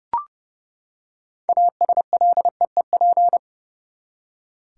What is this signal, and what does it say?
Morse code, heard as a keyed tone, about 31 wpm, 700 Hz. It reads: ASLEEP
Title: Six Letter Words - 30wpm